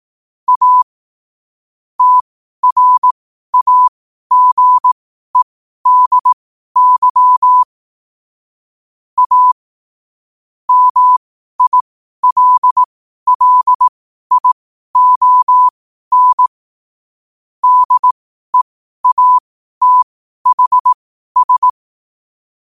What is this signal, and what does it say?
Morse code, heard as a keyed tone, about 18 wpm, spaced as for 12 wpm, 1000 Hz. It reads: A TRAGEDY A MILLION DEATHS
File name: qotdfiles / 12wpm